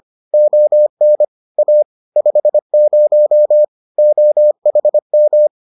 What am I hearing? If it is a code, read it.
Morse code, heard as a keyed tone, about 25 wpm, 600 Hz. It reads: ON A 50 OHM